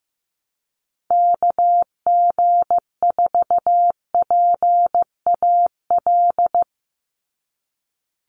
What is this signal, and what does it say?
Morse code, heard as a keyed tone, about 15 wpm, 700 Hz. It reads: KG4PAL
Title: USA Call Signs 15wpm